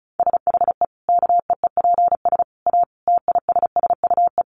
Morse code, heard as a keyed tone, about 34 wpm, 700 Hz. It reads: SHE KEEPS A TISSUE
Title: Easy Phrases 35wpm